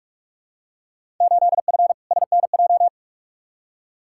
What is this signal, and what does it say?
Morse code, heard as a keyed tone, about 47 wpm, 700 Hz. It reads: 8F SNJ
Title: Field Day Exchanges Encore - 45wpm